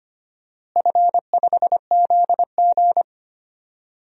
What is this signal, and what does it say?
Morse code, heard as a keyed tone, about 25 wpm, 700 Hz. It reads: F5ZG
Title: DXCC-100 Call Signs - 4 Characters - Repeated 2x - 25wpm